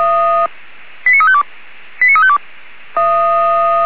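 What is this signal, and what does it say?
Five-tone selective call (ZVEI), heard as quick runs of five tones, tones of about 70 ms. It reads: 89352 89352
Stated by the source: ZVEI five-tone selective call received on an SDR, fire brigade Orlamünde (feuerwehr-orlamuende).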